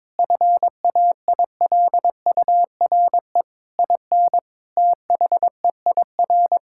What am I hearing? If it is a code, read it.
Morse code, heard as a keyed tone, about 22 wpm, 700 Hz. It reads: FAILURE IN THEIR
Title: Sentences from Top 1000 Words - Review Part 6 - 22wpm